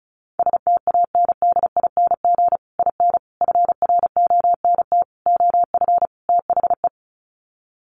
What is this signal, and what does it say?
Morse code, heard as a keyed tone, about 34 wpm, 700 Hz. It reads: STANDING IN FRONT OF THE
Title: Sets of 5 Words Encore2 35wpm